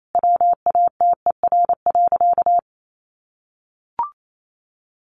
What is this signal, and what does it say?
Morse code, heard as a keyed tone, about 28 wpm, 700 Hz. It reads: WATER.
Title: A Sound Of Thunder 28wpm